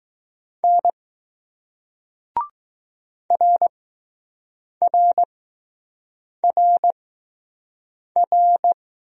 Morse code, heard as a keyed tone, about 20 wpm, spaced as for 6 wpm, 700 Hz. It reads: N RRRR